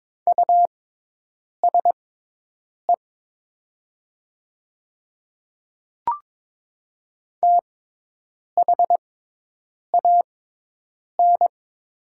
Morse code, heard as a keyed tone, about 22 wpm, spaced as for 8 wpm, 690 Hz. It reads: USE THAN
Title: 6x Character Spacing - Top 100 Words - 22wpm